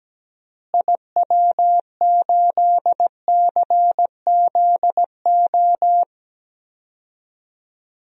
Morse code, heard as a keyed tone, about 17 wpm, 700 Hz. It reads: IW8CZO
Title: DXCC-100 Call Signs - 6 Characters - Repeated 2x - 17wpm